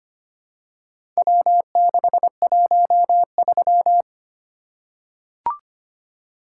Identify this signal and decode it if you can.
Morse code, heard as a keyed tone, about 25 wpm, 700 Hz. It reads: W613
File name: practice / 25wpm / Four Letter-Number Combinations - 25wpm